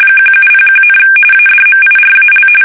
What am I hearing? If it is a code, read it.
RTTY, about 45.45 baud. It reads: 1701, ADMIRAL
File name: fsk850 / battleshipnj